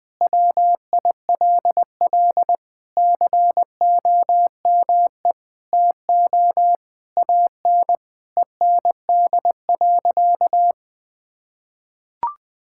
Morse code, heard as a keyed tone, about 20 wpm, 700 Hz. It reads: WILL COME TO AN END.
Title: Sentences from Top 1000 Words - Review Part 6 - 20wpm